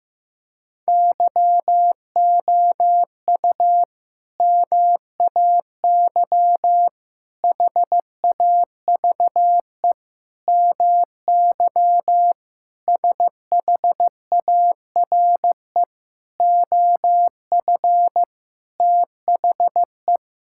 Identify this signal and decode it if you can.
Morse code, heard as a keyed tone, about 15 wpm, 700 Hz. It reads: YOU MAY HAVE MY SHARE OF THE
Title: Sentences from Top 1000 Words - Review Part 6 - 15wpm